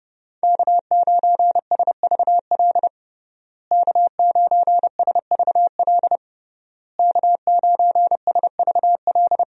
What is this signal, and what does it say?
Morse code, heard as a keyed tone, about 30 wpm, 700 Hz. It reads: K9SVL K9SVL K9SVL
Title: USA Call Signs - 5 Characters - Repeated 3x - 30wpm